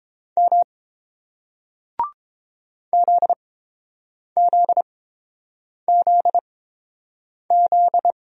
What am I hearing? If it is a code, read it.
Morse code, heard as a keyed tone, about 30 wpm, spaced as for 7 wpm, 690 Hz. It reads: M ZZZZ